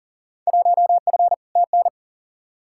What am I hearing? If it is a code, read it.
Morse code, heard as a keyed tone, about 37 wpm, 700 Hz. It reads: 1F TN
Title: Field Day Exchanges - 40wpm